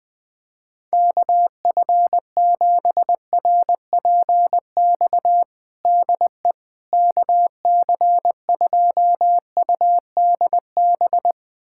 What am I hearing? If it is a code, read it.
Morse code, heard as a keyed tone, about 21 wpm, 700 Hz. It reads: KF7RPX DE KC2UDB
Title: USA Callsign DE Callsign - 6 Characters - 20wpm